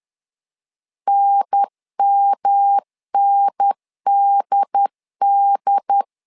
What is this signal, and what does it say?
Morse code, heard as a keyed tone, about 10 wpm, 800 Hz. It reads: NMNDD
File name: LEZIONI / 3 MWRO / 3 M 10wpm 1828